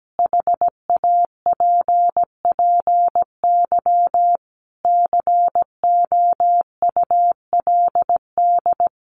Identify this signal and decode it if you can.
Morse code, heard as a keyed tone, about 17 wpm, 700 Hz. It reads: HAPPY COULD
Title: Sentences from Top 1000 Words 17wpm